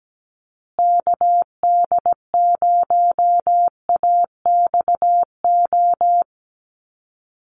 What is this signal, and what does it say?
Morse code, heard as a keyed tone, about 17 wpm, 700 Hz. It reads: KD0AXO
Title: USA Call Signs Repeated 2x 17wpm